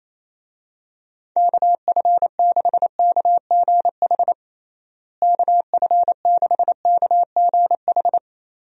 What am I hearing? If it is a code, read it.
Morse code, heard as a keyed tone, about 28 wpm, 700 Hz. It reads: KF6KGH KF6KGH
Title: USA Call Signs - 6 Characters - Repeated 3x - 28wpm